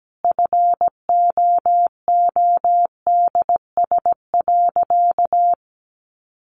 Morse code, heard as a keyed tone, about 17 wpm, 700 Hz. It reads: FOODS.
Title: Easy Phrases 17wpm